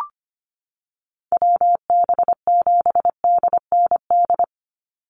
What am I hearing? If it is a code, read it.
Morse code, heard as a keyed tone, about 25 wpm, 700 Hz. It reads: WB7DND
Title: Call Signs Repeated 2x - 25wpm